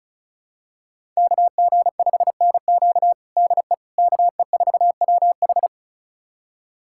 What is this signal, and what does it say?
Morse code, heard as a keyed tone, about 34 wpm, 700 Hz. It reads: KG5NQ DE KE4WH
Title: USA Callsign DE Callsign 35wpm